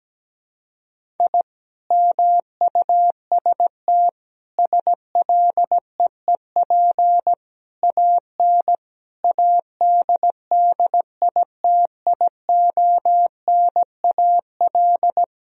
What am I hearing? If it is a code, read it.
Morse code, heard as a keyed tone, about 17 wpm, 700 Hz. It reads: I MUST SLEEP AN ADDITIONAL
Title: Sentences from Top 1000 Words - Review Part 4 - 17wpm